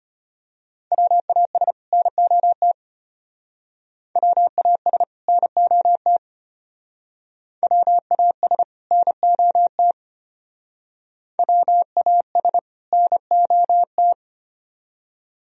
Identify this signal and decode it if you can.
Morse code, heard as a keyed tone, about 32 wpm, 700 Hz. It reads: WAS NOT WAS NOT WAS NOT WAS NOT